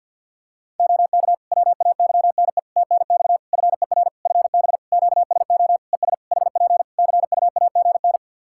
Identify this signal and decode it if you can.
Morse code, heard as a keyed tone, about 51 wpm, 700 Hz. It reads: OK WAYNE TNX FER FB QSO ES HP CUAGN
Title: Practice QSO - Exchange 4 - Ending - 50wpm